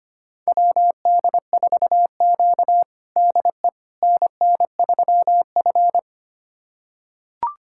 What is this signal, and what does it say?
Morse code, heard as a keyed tone, about 25 wpm, 700 Hz. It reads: WD4Q DE NN3F
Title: USA Callsign DE Callsign - 4 Characters - 25wpm